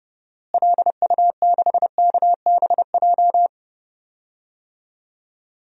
Morse code, heard as a keyed tone, about 30 wpm, 700 Hz. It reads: LU6KBJ
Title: DXCC-100 Call Signs - 6 Characters - Repeated 3x - 30wpm